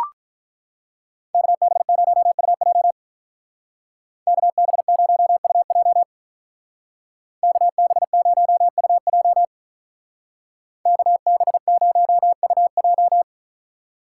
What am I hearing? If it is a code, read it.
Morse code, heard as a keyed tone, about 44 wpm, 700 Hz. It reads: KB0UJ KB0UJ KB0UJ KB0UJ
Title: Speed Racing - USA Call Signs 35wpm